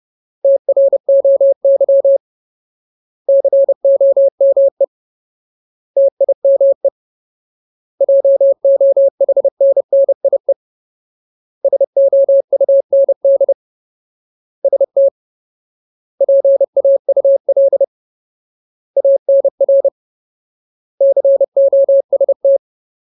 Morse code, heard as a keyed tone, about 30 wpm, 550 Hz. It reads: TROY COME TIME JOHNNIE SOUND ST PAUL ANR COST
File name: Ultimate Word List / 30WPM Ultimate